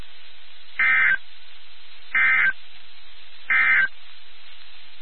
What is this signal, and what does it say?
Emergency Alert System (SAME) tones, heard as harsh buzzing bursts. It is heard from 0.7 to 4.0 s.
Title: NOAA Weather Radio WXK79